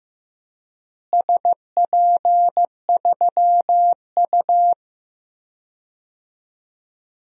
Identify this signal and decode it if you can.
Morse code, heard as a keyed tone, about 15 wpm, 700 Hz. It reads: SP3U
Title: DXCC-100 Call Signs - 4 Characters - Repeated 2x - 15wpm